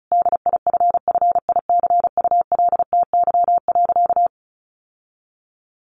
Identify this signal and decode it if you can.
Morse code, heard as a keyed tone, about 34 wpm, 700 Hz. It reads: DIFFICULTY.
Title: Sentences from Top 1500 Words 35wpm